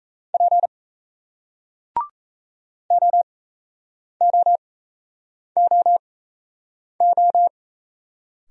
Morse code, heard as a keyed tone, about 37 wpm, spaced as for 8 wpm, 700 Hz. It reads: P OOOO